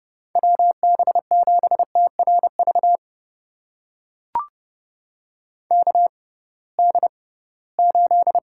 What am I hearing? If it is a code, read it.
Morse code, heard as a keyed tone, about 29 wpm, spaced as for 19 wpm, 700 Hz. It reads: WB7TRV KD8
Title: USA Call Signs - 6 Characters - 6x Character Spacing - 30wpm